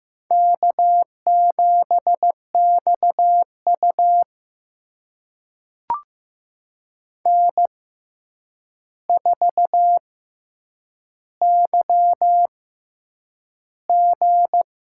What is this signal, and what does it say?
Morse code, heard as a keyed tone, about 15 wpm, 700 Hz. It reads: K 7 X U N 4 Y G
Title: USA Call Signs - 4 Characters - 6x Character Spacing - 15wpm